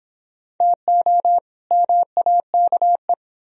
Morse code, heard as a keyed tone, about 31 wpm, 700 Hz. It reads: TO MAKE